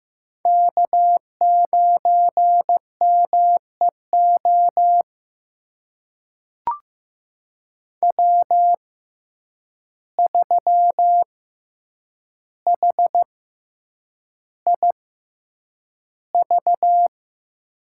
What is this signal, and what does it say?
Morse code, heard as a keyed tone, about 15 wpm, spaced as for 8 wpm, 700 Hz. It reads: K9MEO W3HIV
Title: USA Call Signs - 5 Characters - 6x Character Spacing - 15wpm